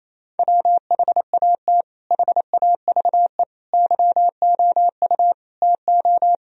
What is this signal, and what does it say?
Morse code, heard as a keyed tone, about 28 wpm, 700 Hz. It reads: WHAT HAVE YOU TO
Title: Sentences from Top 1000 Words - Review Part 3 - 28wpm